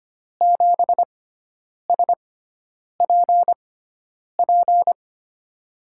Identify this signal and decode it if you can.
Morse code, heard as a keyed tone, about 25 wpm, spaced as for 9 wpm, 700 Hz. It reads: 7SPP
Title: USA Call Signs - 6x Character Spacing - 25wpm